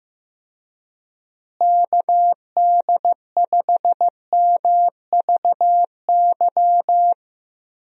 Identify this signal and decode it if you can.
Morse code, heard as a keyed tone, about 15 wpm, 700 Hz. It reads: KD5MVY
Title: USA Call Signs - 6 Characters - Repeated 2x - 15wpm